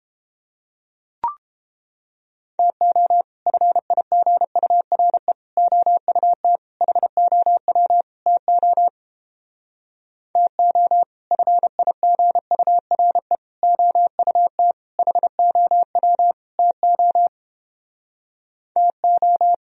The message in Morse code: TO FIGURE OUT HOW TO TO FIGURE OUT HOW TO TO

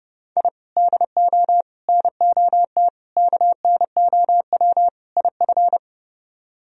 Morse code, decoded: I DO NOT KNOW IF

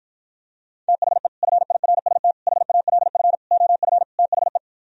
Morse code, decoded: THE FIRST HALF OF THE

